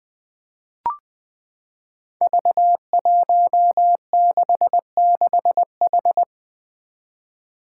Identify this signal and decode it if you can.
Morse code, heard as a keyed tone, about 20 wpm, 700 Hz. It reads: V166H
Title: Five Challenging Letter-Number Combinations 20wpm